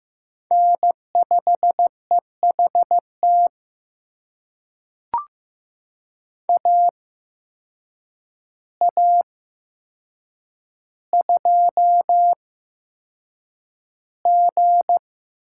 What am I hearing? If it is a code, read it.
Morse code, heard as a keyed tone, about 15 wpm, spaced as for 7 wpm, 700 Hz. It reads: N5EHT AA2G
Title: USA Call Signs - 5 Characters - 8x Character Spacing - 15wpm